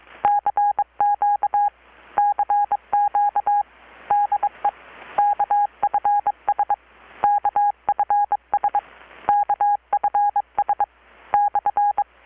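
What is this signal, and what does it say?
Morse code, heard as a keyed tone, about 22 wpm, 820 Hz. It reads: CQ CQ DE KFS KFS KFS/
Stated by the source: Morse from KPH's Night of Nights XX maritime radiotelegraphy event, received in Chiapas, Mexico about 3,800 km away on a Kenwood R-600 with a grounded 10 m long-wire antenna.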